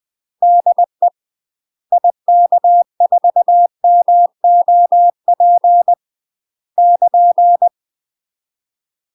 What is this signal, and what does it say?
Morse code, heard as a keyed tone, about 20 wpm, 700 Hz. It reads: DE IK4MOP <KN>
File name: UK MASTER QSOs TRAINING / 20WPM / UK MASTER QSOs 0048